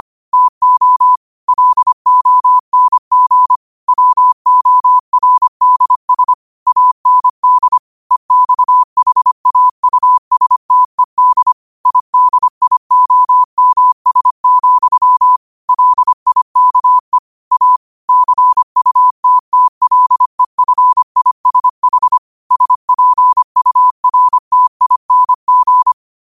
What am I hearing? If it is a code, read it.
Morse code, heard as a keyed tone, about 25 wpm, 1000 Hz. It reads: TO LONG WORDS AND EXHAUSTED IDIOMS, LIKE A CUTTLEFISH SPURTING